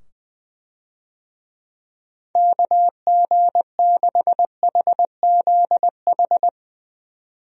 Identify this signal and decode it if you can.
Morse code, heard as a keyed tone, about 21 wpm, 700 Hz. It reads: KG6HZH
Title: USA Call Signs - 6 Characters - 20wpm